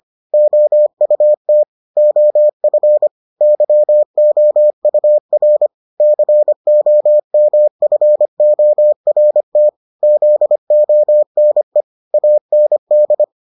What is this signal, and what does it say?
Morse code, heard as a keyed tone, about 25 wpm, 600 Hz. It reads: OUT OF YOUR COMFORT ZONE AND